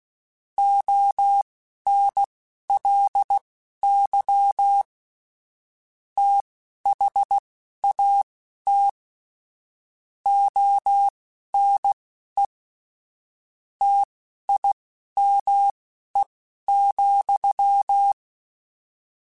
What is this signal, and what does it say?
Morse code, heard as a keyed tone, about 16 wpm, spaced as for 11 wpm, 800 Hz. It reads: ONLY THAT ONE TIME,
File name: Technical Subjects / Technical Subjects-mp3s / AND Gate @10